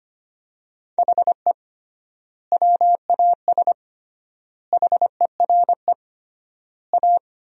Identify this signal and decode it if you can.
Morse code, heard as a keyed tone, about 25 wpm, 700 Hz. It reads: HE WAS HERE A
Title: 3x Word-Spacing - Sentences from Top 500 Words - 25wpm